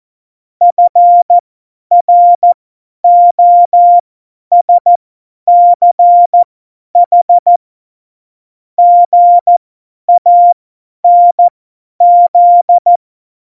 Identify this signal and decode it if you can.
Morse code, heard as a keyed tone, about 14 wpm, spaced as for 10 wpm, 700 Hz. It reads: FROSCH GANZ
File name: Morsekurs / Witze / Witz 27 March 2018, 50 F70